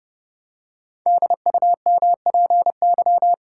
Morse code, heard as a keyed tone, about 30 wpm, 700 Hz. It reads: DUMPY
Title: Pangram Challenge v2 - 30wpm